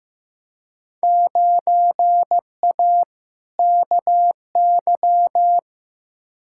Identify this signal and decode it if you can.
Morse code, heard as a keyed tone, about 15 wpm, 700 Hz. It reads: 9A KY
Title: Field Day Exchanges Encore - 15wpm